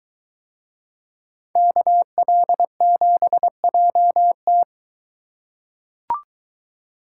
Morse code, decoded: KL7JT

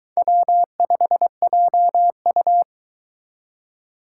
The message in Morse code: W5JU

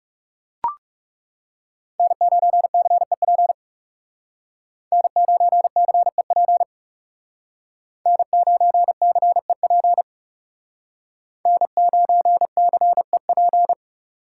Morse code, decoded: N9CEP N9CEP N9CEP N9CEP

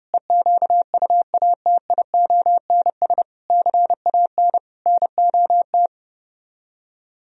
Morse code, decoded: EQUATIONS CAN NOT

